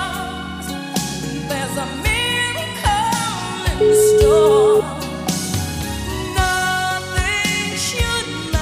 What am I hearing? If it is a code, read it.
Ringback tone (Europe, 425 Hz), with noise.